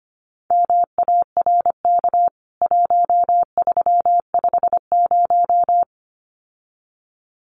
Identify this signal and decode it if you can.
Morse code, heard as a keyed tone, about 25 wpm, 700 Hz. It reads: MARK 1350
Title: CWT Exchange 25wpm